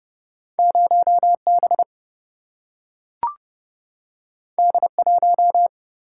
Morse code, heard as a keyed tone, about 29 wpm, 700 Hz. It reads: 0 B D 1
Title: Single-Letter Single-Number 30wpm